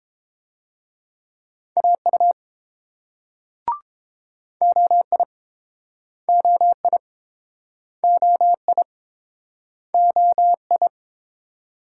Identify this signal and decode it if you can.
Morse code, heard as a keyed tone, about 28 wpm, 700 Hz. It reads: AU OI OI OI OI